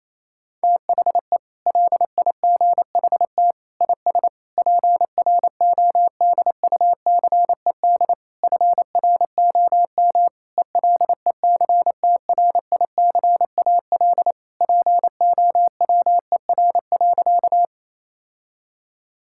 Morse code, heard as a keyed tone, about 28 wpm, 700 Hz. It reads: THE LIGHT IS PRODUCED FROM ELECTRICAL POWER.
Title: Sentences from Top 1000 Words - Review Part 6 - 28wpm